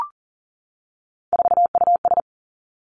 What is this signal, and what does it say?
Morse code, heard as a keyed tone, about 39 wpm, 700 Hz. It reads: 4US
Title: Three Challenging Letter-Number Combinations 40wpm